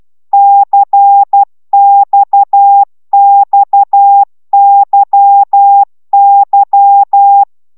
Morse code, heard as a keyed tone, about 12 wpm, 800 Hz. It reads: CXXYY